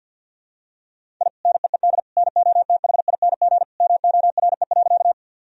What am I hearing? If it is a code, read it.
Morse code, heard as a keyed tone, about 52 wpm, 700 Hz. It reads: I NEED NOTHING MORE.
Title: Sentences from Top 1000 Words - Review Part 1 - 50wpm